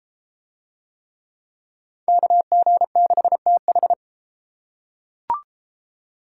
Morse code, heard as a keyed tone, about 33 wpm, 700 Hz. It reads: KG6TH